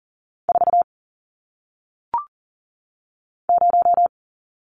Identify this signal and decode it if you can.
Morse code, heard as a keyed tone, about 39 wpm, 700 Hz. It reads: 4 0
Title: Numbers - 1-Digit 40wpm